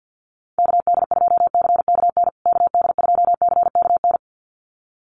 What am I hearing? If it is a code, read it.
Morse code, heard as a keyed tone, about 49 wpm, 700 Hz. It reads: KD2CKN KD2CKN
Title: USA Call Signs Repeated 2x 50wpm